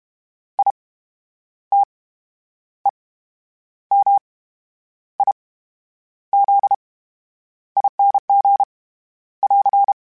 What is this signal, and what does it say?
Morse code, heard as a keyed tone, about 32 wpm, 800 Hz. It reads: I T E M I Z ING <AR>
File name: Special INGs / Special INGs-mp3s / VerySlowToFaster Ing @30